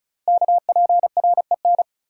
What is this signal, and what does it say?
Morse code, heard as a keyed tone, about 36 wpm, 690 Hz. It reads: KPREN
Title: Code Groups - 5 Letters - 35wpm